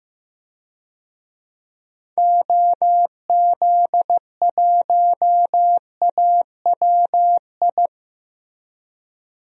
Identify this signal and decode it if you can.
Morse code, heard as a keyed tone, about 15 wpm, 700 Hz. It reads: OZ1AWI
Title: DXCC-100 Call Signs - 6 Characters - Repeated 3x - 15wpm